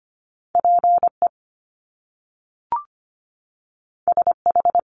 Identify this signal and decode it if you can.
Morse code, heard as a keyed tone, about 25 wpm, 700 Hz. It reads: PE SH
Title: Top Two Letter Combinations in Words 25wpm